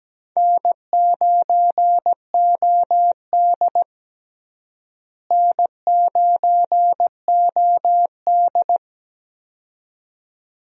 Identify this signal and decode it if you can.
Morse code, heard as a keyed tone, about 17 wpm, 700 Hz. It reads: N9OD N9OD
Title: USA Call Signs - 4 Characters - Repeated 2x - 17wpm